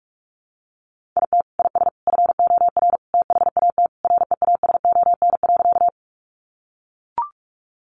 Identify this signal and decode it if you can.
Morse code, heard as a keyed tone, about 44 wpm, 700 Hz. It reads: IT IS FOR THAT REASON.
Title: Sentences from Top 700 Words 45wpm